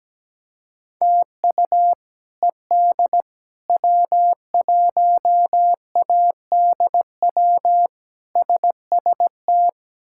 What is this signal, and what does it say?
Morse code, heard as a keyed tone, about 17 wpm, 700 Hz. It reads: TU ED W1ADW SST
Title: K1USN SST QSO Practice - 17wpm